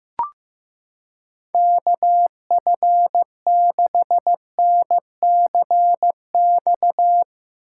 Morse code, heard as a keyed tone, about 15 wpm, 700 Hz. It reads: KF6NCX
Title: K1USN SST QSO Practice - 15wpm